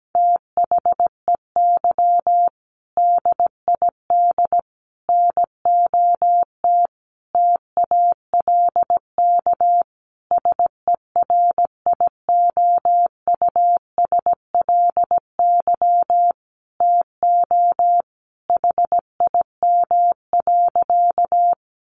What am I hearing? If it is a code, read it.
Morse code, heard as a keyed tone, about 17 wpm, 700 Hz. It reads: THEY DID NOT TALK SERIOUSLY TO HIM.
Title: Sentences from Top 1200 Words 17wpm